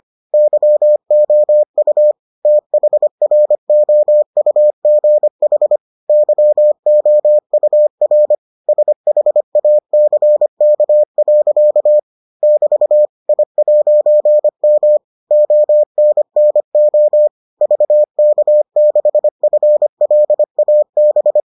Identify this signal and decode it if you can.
Morse code, heard as a keyed tone, about 25 wpm, 600 Hz. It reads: YOU THROUGH YOUR SHACK. <BT> I'M ONNO VK6FLAB